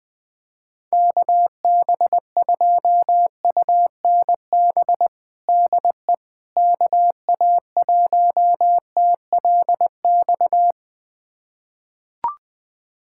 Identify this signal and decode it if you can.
Morse code, heard as a keyed tone, about 20 wpm, 700 Hz. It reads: KB2UNB DE KA1TLX
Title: USA Callsign DE Callsign - 6 Characters - 20wpm